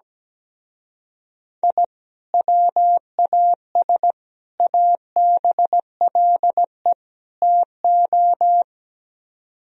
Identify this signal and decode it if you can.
Morse code, heard as a keyed tone, about 17 wpm, 700 Hz. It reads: I WAS ABLE TO
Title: Speed-Racing - Sets of 4 Words 17wpm